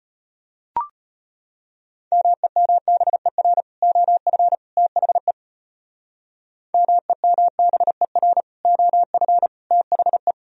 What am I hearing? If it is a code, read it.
Morse code, heard as a keyed tone, about 33 wpm, 700 Hz. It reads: MEMBER OF THE MEMBER OF THE